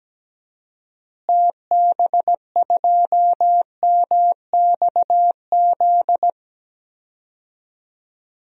Morse code, heard as a keyed tone, about 17 wpm, 700 Hz. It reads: TB2MXZ
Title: DXCC-100 Call Signs - 6 Characters - Repeated 2x - 17wpm